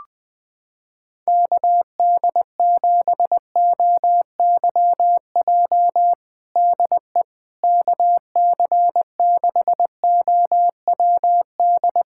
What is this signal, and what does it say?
Morse code, heard as a keyed tone, about 20 wpm, 700 Hz. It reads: KD7OYJ DE KC6OWD
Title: USA Callsign DE Callsign - 6 Characters - 20wpm